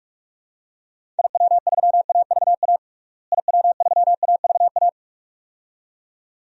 Morse code, heard as a keyed tone, about 48 wpm, 700 Hz. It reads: IW3AVA IW3AVA
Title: DXCC-100 Call Signs - 6 Characters - Repeated 2x - 45wpm